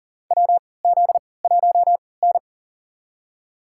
Morse code, heard as a keyed tone, about 40 wpm, spaced as for 23 wpm, 700 Hz. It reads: WZ1N
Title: USA Call Signs - 4 Characters - 3x Character Spacing - 40wpm